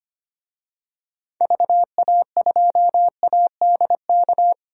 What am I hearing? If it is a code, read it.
Morse code, heard as a keyed tone, about 25 wpm, 700 Hz. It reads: VA2ADK
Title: DXCC-100 Call Signs - 6 Characters - 25wpm